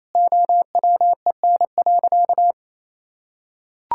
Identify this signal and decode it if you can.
Morse code, heard as a keyed tone, about 28 wpm, 700 Hz. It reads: OWEN.
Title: Pangram Challenge v2 - 28wpm